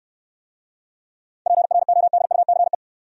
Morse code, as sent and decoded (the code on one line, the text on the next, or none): .--. .-. --- -.. ..- -.-. .
PRODUCE